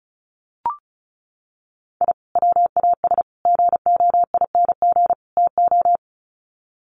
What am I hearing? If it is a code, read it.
Morse code, heard as a keyed tone, about 34 wpm, 700 Hz. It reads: I WAS GOING TO
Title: Sets of 4 Words 35wpm